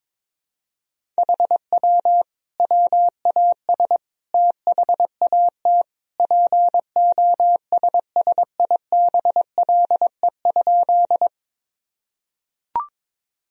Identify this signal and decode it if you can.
Morse code, heard as a keyed tone, about 22 wpm, 700 Hz. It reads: HW WAS THAT POSSIBLE?
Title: Sentences from Top 500 Words and Abbreviations 22wpm